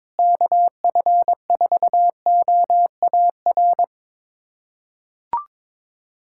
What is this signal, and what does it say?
Morse code, heard as a keyed tone, about 22 wpm, 700 Hz. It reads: KF4OAR
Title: USA Callsign DE Callsign - 6 Characters - 22wpm